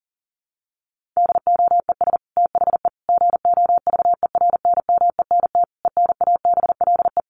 Morse code, heard as a keyed tone, about 40 wpm, 700 Hz. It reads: DOES THE GOVERNMENT ENABLE
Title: Sentences from Top 1500 Words 40wpm